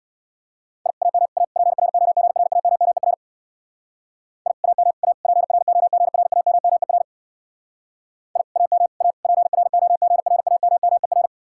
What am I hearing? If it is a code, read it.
Morse code, heard as a keyed tone, about 65 wpm, 700 Hz. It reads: I AM A PROGRAMMER I AM A PROGRAMMER I AM A PROGRAMMER